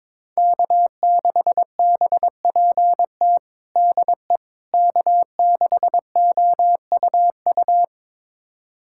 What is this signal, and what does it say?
Morse code, heard as a keyed tone, about 22 wpm, 700 Hz. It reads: K6BPT DE K6OUU
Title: USA Callsign DE Callsign - 5 Characters - 22wpm